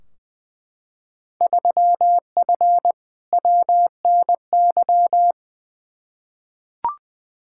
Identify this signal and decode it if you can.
Morse code, heard as a keyed tone, about 21 wpm, 700 Hz. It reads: 3F WNY
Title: Field Day Exchanges - 20wpm